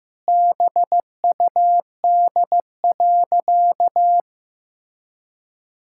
Morse code, heard as a keyed tone, about 15 wpm, 700 Hz. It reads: BUD.
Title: Pangram Challenge v2 - 15wpm